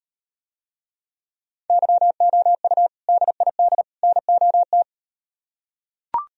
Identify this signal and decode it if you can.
Morse code, heard as a keyed tone, about 38 wpm, 700 Hz. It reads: YOU DID NOT